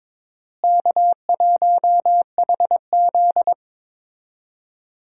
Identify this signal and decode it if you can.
Morse code, heard as a keyed tone, about 22 wpm, 700 Hz. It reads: K1HZ
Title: USA Call Signs - 4 Characters - Repeated 2x - 22wpm